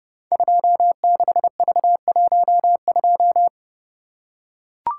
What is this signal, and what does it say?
Morse code, heard as a keyed tone, about 30 wpm, 700 Hz. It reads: 26V12
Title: Code Groups - 5 Letter:Numbers - 30wpm